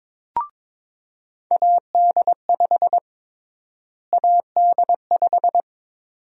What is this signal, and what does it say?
Morse code, heard as a keyed tone, about 22 wpm, 700 Hz. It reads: AD5 AD5
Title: USA Call Sign Prefixes - Repeated 2x - 22wpm